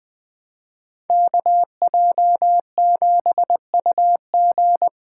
Morse code, heard as a keyed tone, about 20 wpm, 700 Hz. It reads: KJ7UG
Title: USA Call Signs - 5 Characters - 4x Character Spacing - 20wpm